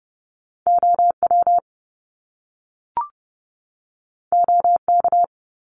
Morse code, heard as a keyed tone, about 30 wpm, 700 Hz. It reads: OW OK
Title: Top Two Letter Combinations in Words 30wpm